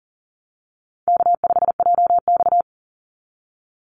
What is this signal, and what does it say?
Morse code, heard as a keyed tone, about 39 wpm, 700 Hz. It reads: K5JX
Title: Easy Call Signs Repeated 2x - 40wpm